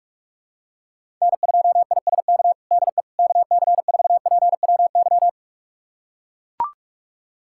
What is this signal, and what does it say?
Morse code, heard as a keyed tone, about 41 wpm, 700 Hz. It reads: N2ISK DE KC4PWY
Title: USA Callsign DE Callsign 45wpm